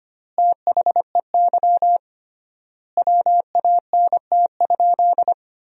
Morse code, heard as a keyed tone, about 25 wpm, 700 Hz. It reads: THEY WANT?
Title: 3x Word-Spacing - Sentences from Top 200 Words - 25wpm